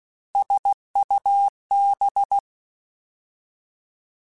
Morse code, heard as a keyed tone, about 16 wpm, 800 Hz. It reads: SUB
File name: Alphabetical / Alphabetical mp3s / Ss @15